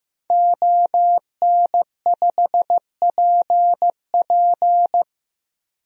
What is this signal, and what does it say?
Morse code, heard as a keyed tone, about 15 wpm, 700 Hz. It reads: ON5PP
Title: DXCC-100 Call Signs - 5 Characters - Repeated 2x - 15wpm